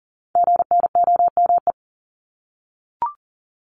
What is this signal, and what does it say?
Morse code, heard as a keyed tone, about 40 wpm, 700 Hz. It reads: GNOME
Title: Multi-letter Phonemes Repeated 3x 40wpm